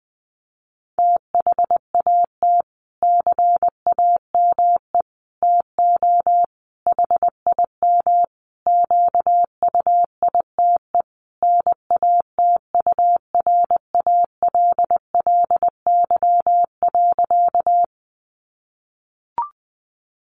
Morse code, heard as a keyed tone, about 20 wpm, 700 Hz. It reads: THAT CAME TO HIM QUITE NATURALLY.
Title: Sentences from Top 1300 Words 20wpm